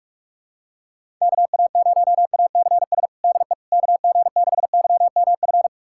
Morse code, heard as a keyed tone, about 46 wpm, 700 Hz. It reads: KA0ACS DE KG6YMF
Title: USA Callsign DE Callsign 45wpm